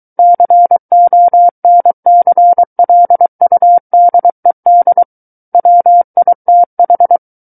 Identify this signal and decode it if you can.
Morse code, heard as a keyed tone, about 23 wpm, 700 Hz. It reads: CONCLUDED WITH
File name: Book One / 14-Chapter 14 - 23 WPM